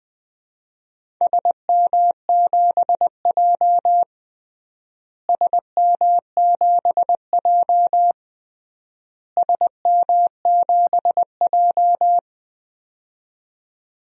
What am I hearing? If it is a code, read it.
Morse code, heard as a keyed tone, about 20 wpm, 700 Hz. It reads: SM7J SM7J SM7J
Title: DXCC-100 Call Signs - 4 Characters - Repeated 3x - 20wpm